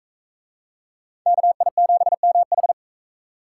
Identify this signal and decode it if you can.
Morse code, heard as a keyed tone, about 40 wpm, 700 Hz. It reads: KI7MH